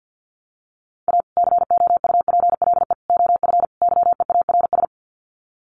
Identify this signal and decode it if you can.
Morse code, heard as a keyed tone, about 48 wpm, 700 Hz. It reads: A COUPLE OF YEARS
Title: Sets of 4 Words 50wpm